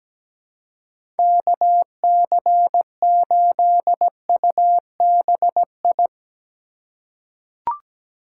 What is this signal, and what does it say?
Morse code, heard as a keyed tone, about 17 wpm, 700 Hz. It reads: KC8UBI
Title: USA Call Signs - 6 Characters - 3x Character Spacing - 17wpm